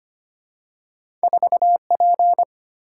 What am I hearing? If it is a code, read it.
Morse code, heard as a keyed tone, about 25 wpm, 700 Hz. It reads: 4P
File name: Two Letter-Number Combinations - 25wpm